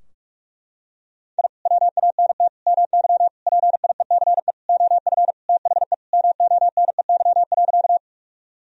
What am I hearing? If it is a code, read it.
Morse code, heard as a keyed tone, about 44 wpm, 700 Hz. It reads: I WANT MY PIECE OF THE MONEY.
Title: Sentences from Top 1000 Words - Review Part 2 - 45wpm